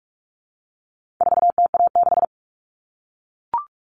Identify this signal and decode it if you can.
Morse code, heard as a keyed tone, about 44 wpm, 700 Hz. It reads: 4TA6
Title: All Cut Numbers - 4-Digit 45wpm